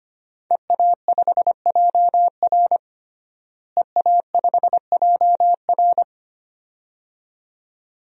Morse code, heard as a keyed tone, about 25 wpm, 700 Hz. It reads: EA5JR EA5JR
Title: DXCC-100 Call Signs - 5 Characters - Repeated 2x - 25wpm